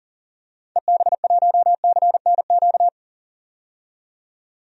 Morse code, heard as a keyed tone, about 40 wpm, 700 Hz. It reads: EB1CNQ
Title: DXCC-100 Call Signs - 6 Characters - 40wpm